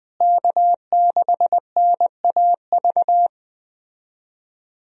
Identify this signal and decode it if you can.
Morse code, heard as a keyed tone, about 20 wpm, 700 Hz. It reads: K6NAV
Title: USA Call Signs - 5 Characters - Repeated 3x - 20wpm